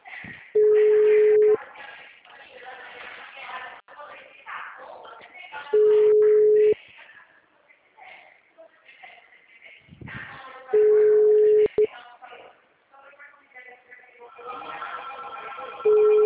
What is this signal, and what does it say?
Ringback tone (Europe, 425 Hz), with speech.
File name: out- -2045- - - .31384